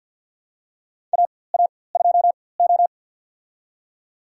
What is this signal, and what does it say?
Morse code, heard as a keyed tone, about 51 wpm, spaced as for 23 wpm, 700 Hz. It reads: AA2O
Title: USA Call Signs - 4x Character Spacing - 50wpm